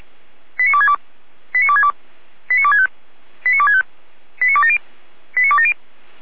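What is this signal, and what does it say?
Five-tone selective call (ZVEI), heard as quick runs of five tones, tones of about 70 ms. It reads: 89262 89262 89265 89265 89270 89270
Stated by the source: ZVEI five-tone selective call received on SDR.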